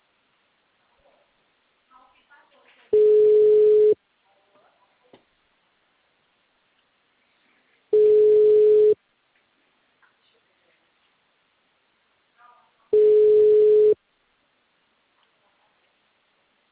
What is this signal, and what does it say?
Ringback tone (Europe, 425 Hz), with speech.